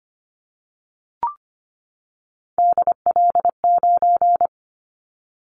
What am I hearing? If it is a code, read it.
Morse code, heard as a keyed tone, about 25 wpm, 700 Hz. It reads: DL9
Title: Callsign Prefixes for Top 25 DXCC - 25wpm